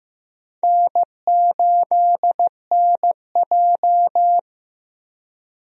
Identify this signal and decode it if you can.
Morse code, heard as a keyed tone, about 15 wpm, 700 Hz. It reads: N8NJ
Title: USA Call Signs - 4 Characters - Repeated 2x - 15wpm